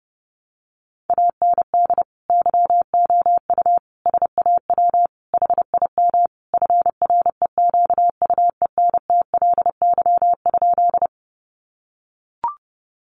Morse code, heard as a keyed tone, about 28 wpm, 700 Hz. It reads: AND YOU SAW HIM FREQUENTLY?
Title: Sentences from Top 1300 Words 30wpm